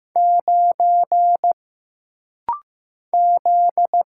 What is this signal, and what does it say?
Morse code, heard as a keyed tone, about 15 wpm, 700 Hz. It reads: 9 Z
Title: Single Letter-Number Alt - Rapid-Fire 15wpm